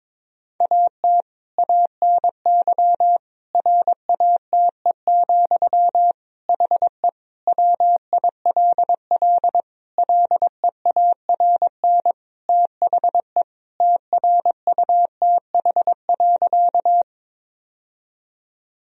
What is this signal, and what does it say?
Morse code, heard as a keyed tone, about 22 wpm, 700 Hz. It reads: AT ANY RATE, HE WILL LEARN THE TRUTH.
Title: Sentences from Top 1000 Words - Review Part 4 - 22wpm